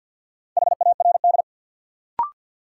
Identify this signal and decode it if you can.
Morse code, heard as a keyed tone, about 53 wpm, 700 Hz. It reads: HARD
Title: Sentences from Top 300 Words - New Word Drill - 50wpm